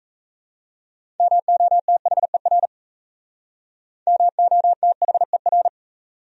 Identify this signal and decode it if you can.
Morse code, heard as a keyed tone, about 35 wpm, 700 Hz. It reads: MOTHER MOTHER
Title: Speed-Racing Top 500 Words 28wpm